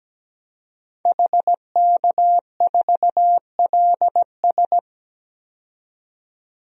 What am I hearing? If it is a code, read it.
Morse code, heard as a keyed tone, about 17 wpm, 700 Hz. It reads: HK4LS
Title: DXCC-100 Call Signs - 5 Characters - 17wpm